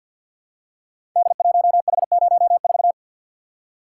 Morse code, heard as a keyed tone, about 48 wpm, 700 Hz. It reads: D1H04
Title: Code Groups - 5 Letter:Numbers - 50wpm